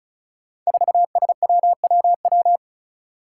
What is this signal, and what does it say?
Morse code, heard as a keyed tone, about 36 wpm, 700 Hz. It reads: 4SWWW
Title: Five Challenging Letter-Number Combinations 35wpm